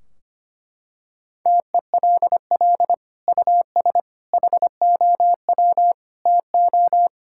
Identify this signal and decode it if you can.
Morse code, heard as a keyed tone, about 25 wpm, 700 Hz. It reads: TELL US HOW TO
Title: Sentences from Top 1600 Words 25wpm